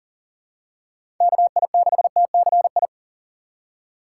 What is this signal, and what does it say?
Morse code, heard as a keyed tone, about 40 wpm, 700 Hz. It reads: KI6TCI
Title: USA Call Signs - 6x Character Spacing - 40wpm